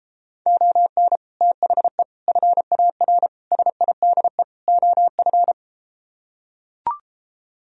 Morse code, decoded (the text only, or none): ON THE FAR SIDE OF